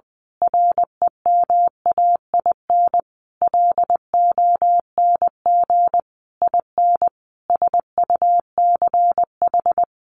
Morse code, heard as a keyed tone, about 20 wpm, 700 Hz. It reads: REMAIN LONG IN SUCH
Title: Sentences from Top 1500 Words 20wpm